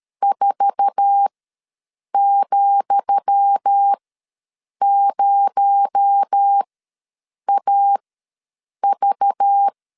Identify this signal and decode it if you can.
Morse code, heard as a keyed tone, about 13 wpm, spaced as for 7 wpm, 800 Hz. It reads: 4,0AV
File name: LEZIONI / 10 SIGNS / 10D COMA ws 12wpm 3618